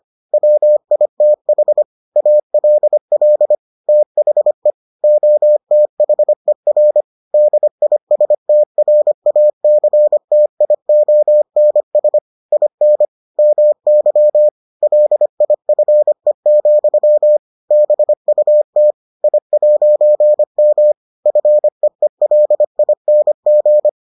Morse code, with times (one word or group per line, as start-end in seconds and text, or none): 0.3-1.8 WITH
2.2-3.5 ALL
3.9-4.7 THE
5.0-7.0 OTHER
7.3-12.2 DISTRACTIONS
12.5-13.1 IN
13.4-14.5 MY
14.8-17.4 LIFE,
17.7-18.9 BUT
19.2-20.9 I'M
21.3-23.9 FEELING